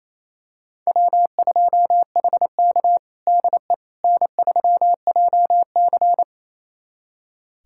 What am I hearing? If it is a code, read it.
Morse code, heard as a keyed tone, about 28 wpm, 700 Hz. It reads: W2HK DE N3JC
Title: USA Callsign DE Callsign - 4 Characters - 28wpm